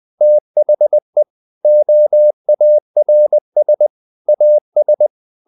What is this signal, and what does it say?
Morse code, heard as a keyed tone, about 20 wpm, 600 Hz. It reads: THE OARS AS
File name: the old man and the sea part 2 20wpm